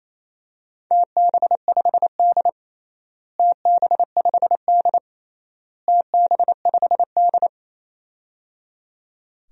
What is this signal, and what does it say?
Morse code, heard as a keyed tone, about 27 wpm, 700 Hz. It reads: TB5D TB5D TB5D
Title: DXCC-100 Call Signs - 4 Characters - Repeated 3x - 28wpm